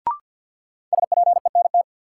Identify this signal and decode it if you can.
Morse code, heard as a keyed tone, about 47 wpm, 700 Hz. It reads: SPENT
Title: Sets of 5 Words - New Word Drill - 50wpm